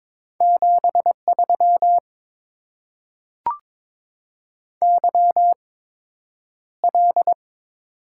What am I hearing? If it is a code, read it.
Morse code, heard as a keyed tone, about 21 wpm, spaced as for 9 wpm, 700 Hz. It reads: 73 YL
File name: Top 90 QSO Elements - 8x Character Spacing - 22wpm